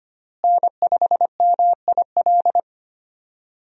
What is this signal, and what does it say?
Morse code, heard as a keyed tone, about 25 wpm, 700 Hz. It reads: N5MIL
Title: USA Call Signs - 5 Characters - Repeated 3x - 25wpm